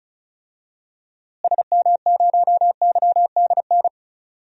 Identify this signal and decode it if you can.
Morse code, heard as a keyed tone, about 34 wpm, 700 Hz. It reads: SM0YDN
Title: DXCC-100 Call Signs - 6 Characters - Repeated 2x - 35wpm